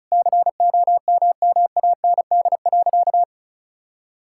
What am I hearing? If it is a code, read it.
Morse code, heard as a keyed tone, about 36 wpm, 700 Hz. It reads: COMMAND.
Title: Sentences from Top 1600 Words 35wpm